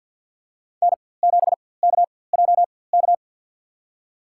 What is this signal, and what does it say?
Morse code, heard as a keyed tone, about 52 wpm, spaced as for 23 wpm, 700 Hz. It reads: N7KJK
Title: USA Call Signs - 5 Characters - 4x Character Spacing - 50wpm